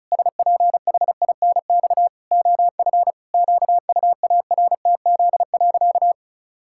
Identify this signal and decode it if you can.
Morse code, heard as a keyed tone, about 34 wpm, 700 Hz. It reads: SPHINX OF QUARTZ.
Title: Pangram Challenge v2 - 35wpm